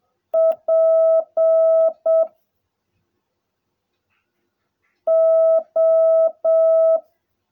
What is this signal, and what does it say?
Morse code, heard as a keyed tone, about 7 wpm, 660 Hz. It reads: P O
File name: jh8cbh / 2021cw / cw16a